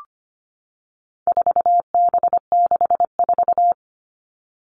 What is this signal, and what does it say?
Morse code, heard as a keyed tone, about 25 wpm, 700 Hz. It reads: 4B64
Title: All Cut Numbers - 4-Digit 25wpm